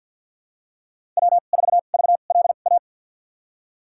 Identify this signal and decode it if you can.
Morse code, heard as a keyed tone, about 52 wpm, spaced as for 36 wpm, 700 Hz. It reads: W4VLA
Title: USA Call Signs - 5 Characters - 2x Character Spacing - 50wpm